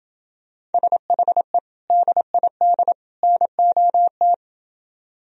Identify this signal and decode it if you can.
Morse code, heard as a keyed tone, about 28 wpm, 700 Hz. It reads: SHE DID NOT